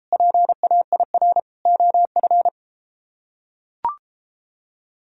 Morse code, heard as a keyed tone, about 31 wpm, 700 Hz. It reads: PAIR OF